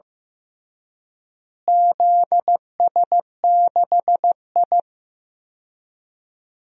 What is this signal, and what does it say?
Morse code, heard as a keyed tone, about 15 wpm, 700 Hz. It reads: ZS6I
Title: DXCC-100 Call Signs - 4 Characters - Repeated 3x - 15wpm